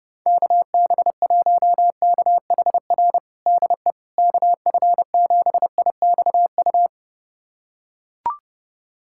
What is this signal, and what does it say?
Morse code, heard as a keyed tone, about 29 wpm, 700 Hz. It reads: KB1KHR DE KF7IXU
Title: USA Callsign DE Callsign - 6 Characters - 30wpm